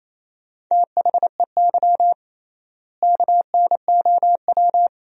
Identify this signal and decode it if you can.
Morse code, heard as a keyed tone, about 28 wpm, 700 Hz. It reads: THEY KNOW
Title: 3x Word-Spacing - Sentences from Top 500 Words - 28wpm